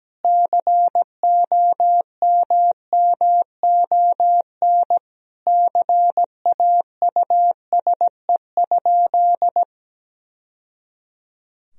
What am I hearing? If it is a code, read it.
Morse code, heard as a keyed tone, about 17 wpm, 700 Hz. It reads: COMMON CAUSE?
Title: Sentences from Top 1000 Words - Review Part 3 - 17wpm